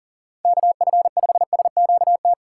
Morse code, heard as a keyed tone, about 40 wpm, 700 Hz. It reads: KF5SQT
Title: USA Call Signs - 6 Characters - 3x Character Spacing - 40wpm